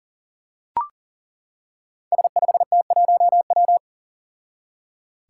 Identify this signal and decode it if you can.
Morse code, heard as a keyed tone, about 39 wpm, 700 Hz. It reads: S5T1W
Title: Five Challenging Letter-Number Combinations 40wpm